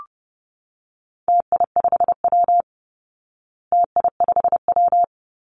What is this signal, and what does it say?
Morse code, heard as a keyed tone, about 30 wpm, 700 Hz. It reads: TI5W TI5W
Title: Call Signs Repeated 2x - 30wpm